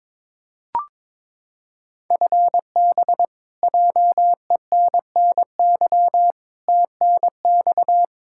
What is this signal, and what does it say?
Morse code, heard as a keyed tone, about 23 wpm, 700 Hz. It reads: FB JENNY TNX
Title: K1USN SST QSO Practice 22wpm